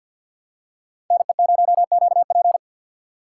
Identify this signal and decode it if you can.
Morse code, heard as a keyed tone, about 49 wpm, 700 Hz. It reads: NE0QP
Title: USA Call Signs - 6x Character Spacing - 50wpm